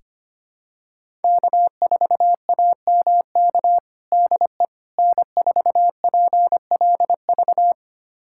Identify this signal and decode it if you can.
Morse code, heard as a keyed tone, about 25 wpm, 700 Hz. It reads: K4AMK DE N4PLV
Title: USA Callsign DE Callsign - 5 Characters - 25wpm